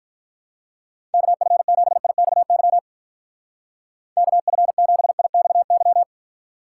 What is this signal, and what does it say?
Morse code, heard as a keyed tone, about 42 wpm, 700 Hz. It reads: KF7IXY KF7IXY
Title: Speed Racing - USA Call Signs 35wpm